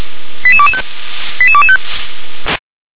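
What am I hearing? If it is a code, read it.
Five-tone selective call (ZVEI), heard as quick runs of five tones, tones of about 70 ms. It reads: 88225 88225
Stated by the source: Five-tone ZVEI selective call received by SDR.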